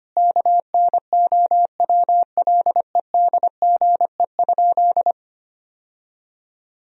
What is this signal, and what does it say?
Morse code, heard as a keyed tone, about 25 wpm, 700 Hz. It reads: KNOWLEDGE?
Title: Sentences from Top 1000 Words - Review Part 3 - 25wpm